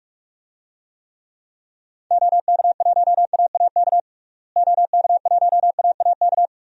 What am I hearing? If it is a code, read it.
Morse code, heard as a keyed tone, about 41 wpm, 700 Hz. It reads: OK1AAK OK1AAK
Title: DXCC-100 Call Signs - 6 Characters - Repeated 2x - 45wpm